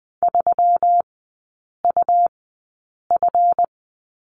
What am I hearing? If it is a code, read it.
Morse code, heard as a keyed tone, about 20 wpm, spaced as for 8 wpm, 700 Hz. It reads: 3UF
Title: 145 Lesson 24e - Letter y - Three Characters 20wpm